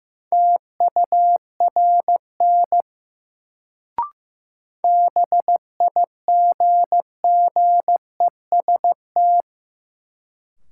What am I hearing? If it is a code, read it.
Morse code, heard as a keyed tone, about 15 wpm, 700 Hz. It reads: TURN BIGGEST
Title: Sets of 4 Words - New Word Drill - 15wpm